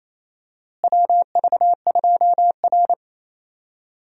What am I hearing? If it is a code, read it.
Morse code, heard as a keyed tone, about 28 wpm, 700 Hz. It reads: WV2R
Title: USA Call Signs - 4 Characters - Repeated 2x - 28wpm